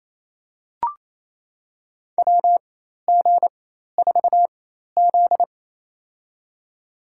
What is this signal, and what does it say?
Morse code, heard as a keyed tone, about 28 wpm, spaced as for 13 wpm, 700 Hz. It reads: WG4Z
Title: USA Call Signs - 4x Character Spacing - 28wpm